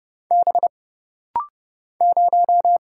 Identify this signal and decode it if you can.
Morse code, heard as a keyed tone, about 30 wpm, 700 Hz. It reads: B 0
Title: Single Letter-Number Alt - Rapid-Fire 30wpm